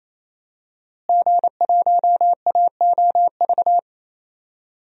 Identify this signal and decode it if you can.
Morse code, heard as a keyed tone, about 27 wpm, 700 Hz. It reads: G1AOV
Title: DXCC-100 Call Signs - 5 Characters - Repeated 3x - 28wpm